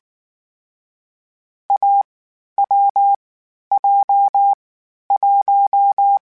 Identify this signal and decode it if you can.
Morse code, heard as a keyed tone, about 19 wpm, 800 Hz. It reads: A W J 1
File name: Difficult Combos / Difficult Combos-mp3s / AWJ1 @18